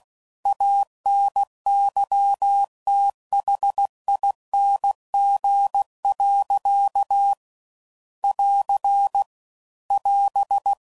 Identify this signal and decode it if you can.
Morse code, heard as a keyed tone, about 16 wpm, 800 Hz. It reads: ANYTHING. <AR> <AS>